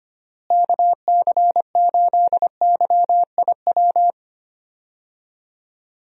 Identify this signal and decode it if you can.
Morse code, heard as a keyed tone, about 25 wpm, 700 Hz. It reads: KC8YIW
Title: USA Call Signs - 6 Characters - Repeated 3x - 25wpm